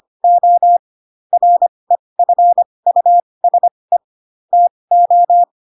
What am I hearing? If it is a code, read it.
Morse code, heard as a keyed tone, about 25 wpm, spaced as for 20 wpm, 700 Hz. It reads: O REFUSE TO
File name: Pinocchio / Pinocchio ch4 15 25 ik1hge